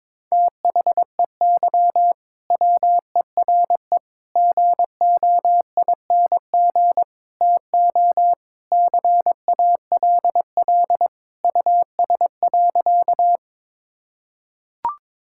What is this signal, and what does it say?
Morse code, heard as a keyed tone, about 22 wpm, 700 Hz. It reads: THEY WERE GOING TO CALL US.
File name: practice / 22wpm / Sentences with 4-Word Phrases - 22wpm